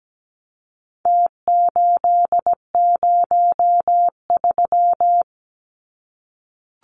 Morse code, heard as a keyed tone, about 17 wpm, 700 Hz. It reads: T803
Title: Numbers - Contest Style - 4-Digit 17wpm